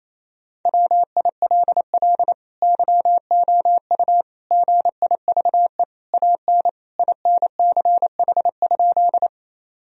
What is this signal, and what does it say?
Morse code, heard as a keyed tone, about 28 wpm, 700 Hz. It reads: WILL YOU GIVE AN INCH?
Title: Sentences from Top 1000 Words - Review Part 3 - 28wpm